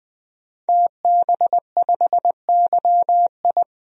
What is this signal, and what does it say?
Morse code, heard as a keyed tone, about 20 wpm, 700 Hz. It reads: TB5YI
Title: DXCC-100 Call Signs - 5 Characters - 20wpm